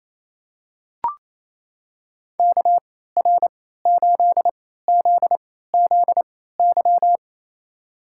Morse code, heard as a keyed tone, about 27 wpm, spaced as for 16 wpm, 700 Hz. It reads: KR8ZZY
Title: USA Call Signs - 6 Characters - 3x Character Spacing - 28wpm